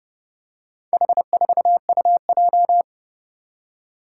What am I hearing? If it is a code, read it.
Morse code, heard as a keyed tone, about 32 wpm, 700 Hz. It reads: H4UJ
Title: Four Challenging Letter-Number Combinations 30wpm